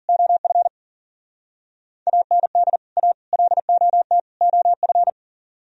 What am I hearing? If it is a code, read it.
Morse code, heard as a keyed tone, about 48 wpm, 700 Hz. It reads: OF AND A LOT OF
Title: Speed-Racing - Sets of 4 Words 40wpm